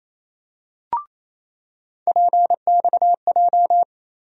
Morse code, heard as a keyed tone, about 28 wpm, 700 Hz. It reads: PXJ
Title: USA Call Sign Postfixes - 3 Letters 28wpm